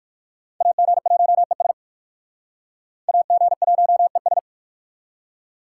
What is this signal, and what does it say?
Morse code, heard as a keyed tone, about 58 wpm, 700 Hz. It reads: AG1ES AG1ES